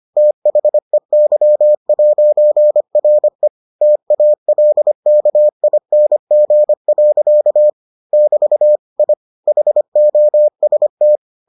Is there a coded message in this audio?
Yes: Morse code from 0.2 to 11.2 s, heard as a keyed tone, about 25 wpm, 600 Hz. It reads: THEY'RE TALKING. <BT> I HOST